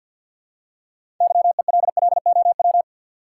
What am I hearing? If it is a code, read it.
Morse code, heard as a keyed tone, about 52 wpm, 700 Hz. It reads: YELLOW
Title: Six Letter Words - 50wpm